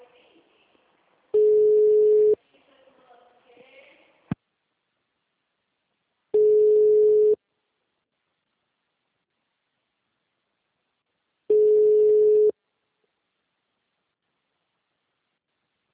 Ringback tone (Europe, 425 Hz).
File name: out- -2015- - - .38768